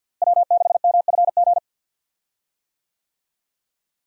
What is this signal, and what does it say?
Morse code, heard as a keyed tone, about 52 wpm, 700 Hz. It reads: W6MFG
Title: USA Call Signs - 5 Characters - Repeated 3x - 50wpm